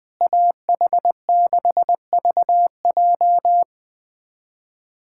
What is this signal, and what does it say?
Morse code, heard as a keyed tone, about 21 wpm, 700 Hz. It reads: AH6VJ
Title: USA Callsign DE Callsign - 5 Characters - 20wpm